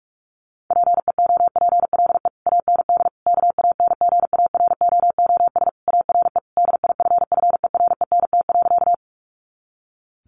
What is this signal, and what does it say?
Morse code, heard as a keyed tone, about 46 wpm, 700 Hz. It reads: PEOPLE AND KANGAROOS ARE DIFFERENT.
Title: Easy Phrases 45wpm